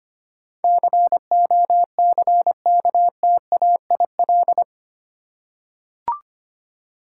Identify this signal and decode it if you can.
Morse code, heard as a keyed tone, about 25 wpm, 700 Hz. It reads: COCKTAIL
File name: 8 Letter Words - First 3 Letters Given - 25wpm